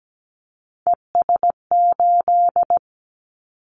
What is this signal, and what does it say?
Morse code, heard as a keyed tone, about 17 wpm, 700 Hz. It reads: ES8
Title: Callsign Prefixes for Top 50 DXCC - 17wpm